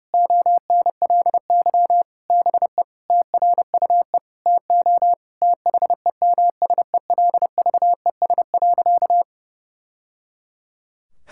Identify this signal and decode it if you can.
Morse code, heard as a keyed tone, about 30 wpm, 700 Hz. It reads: ONLY BE TRUE TO THEMSELVES.
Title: Sentences from Top 1000 Words - Review Part 6 - 30wpm